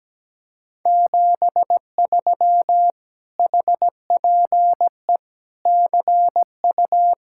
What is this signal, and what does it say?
Morse code, heard as a keyed tone, about 17 wpm, 700 Hz. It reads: 73 HPE CU
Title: Speed-Racing Common QSO Phrases Encore - 17wpm